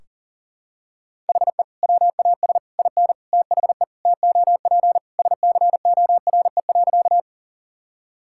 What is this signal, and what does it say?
Morse code, heard as a keyed tone, about 40 wpm, 700 Hz. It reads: HE WAS IN THE TOP SCORE.
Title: Sentences from Top 1600 Words 40wpm